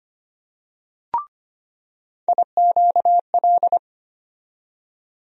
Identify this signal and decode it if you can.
Morse code, heard as a keyed tone, about 25 wpm, 700 Hz. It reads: IQL
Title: USA Call Sign Postfixes - 3 Letters 25wpm